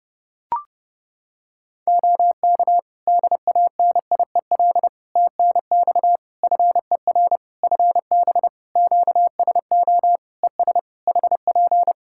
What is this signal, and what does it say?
Morse code, heard as a keyed tone, about 30 wpm, 700 Hz. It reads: OK DANIEL TNX FER FB QSO ES HP
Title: K1USN SST QSO Practice 30wpm